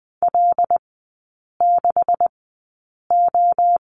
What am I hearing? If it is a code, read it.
Morse code, heard as a keyed tone, about 20 wpm, spaced as for 8 wpm, 690 Hz. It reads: L6O
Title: 172 Lesson 28e - Number 9 - Three Characters 20wpm